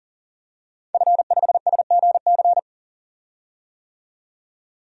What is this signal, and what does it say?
Morse code, heard as a keyed tone, about 40 wpm, 700 Hz. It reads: F5SGC
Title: DXCC-100 Call Signs - 5 Characters - 40wpm